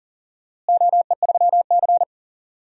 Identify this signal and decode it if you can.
Morse code, heard as a keyed tone, about 40 wpm, 700 Hz. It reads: OE3C
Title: DXCC-100 Call Signs - 4 Characters - Repeated 3x - 40wpm